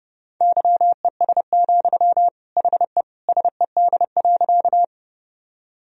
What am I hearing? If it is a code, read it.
Morse code, heard as a keyed tone, about 30 wpm, 700 Hz. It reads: YES, HE SED.
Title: Sentences from Top 500 Words and Abbreviations 30wpm